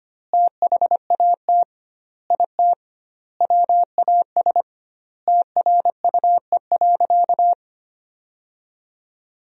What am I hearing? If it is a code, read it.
Morse code, heard as a keyed tone, about 25 wpm, 700 Hz. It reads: THAT IT WAS TRUE.
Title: 2x Word-Spacing - Sentences from Top 400 Words - 25wpm